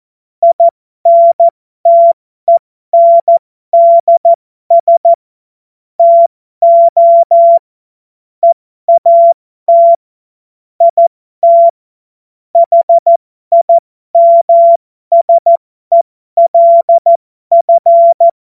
Morse code, heard as a keyed tone, about 14 wpm, spaced as for 12 wpm, 680 Hz. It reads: INTENDS TO EAT IT HIMSELF
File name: Morsekurs / Witze